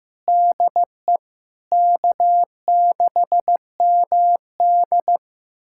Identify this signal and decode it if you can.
Morse code, heard as a keyed tone, about 16 wpm, 700 Hz. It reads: DE K6MD
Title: USA Callsign DE Callsign - 4 Characters - 15wpm